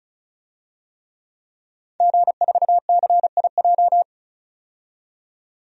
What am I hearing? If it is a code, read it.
Morse code, heard as a keyed tone, about 34 wpm, 700 Hz. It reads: G4CIJ
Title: DXCC-100 Call Signs - 5 Characters - 35wpm